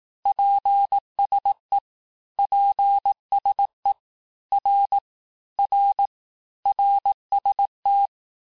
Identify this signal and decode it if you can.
Morse code, heard as a keyed tone, about 18 wpm, 800 Hz. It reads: PSE PSE R R RST
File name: cours-du-lundi files / cours-1588-texte-abreviations-vitesse-17mpm-puis-22mpm